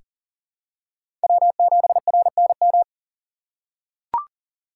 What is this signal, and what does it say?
Morse code, heard as a keyed tone, about 40 wpm, 700 Hz. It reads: W7RNM
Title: USA Call Signs - 5 Characters - 6x Character Spacing - 40wpm